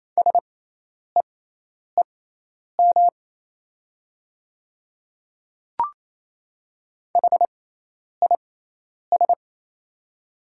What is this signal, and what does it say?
Morse code, heard as a keyed tone, about 28 wpm, spaced as for 10 wpm, 700 Hz. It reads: SEEM HIS
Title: 6x Character Spacing - Top 300 Words - 28wpm